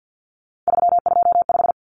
Morse code, heard as a keyed tone, about 50 wpm, 700 Hz. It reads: 325
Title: Common Cut Numbers - 3-Digit 50wpm